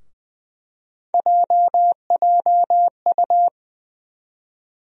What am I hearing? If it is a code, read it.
Morse code, heard as a keyed tone, about 20 wpm, 700 Hz. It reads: JJU
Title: USA Call Sign Postfixes - 3 Letters 20wpm